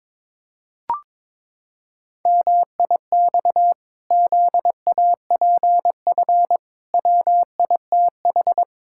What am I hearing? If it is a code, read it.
Morse code, heard as a keyed tone, about 22 wpm, 700 Hz. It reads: MIX ZAPF WITH
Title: Pangram Challenge v2 - 22wpm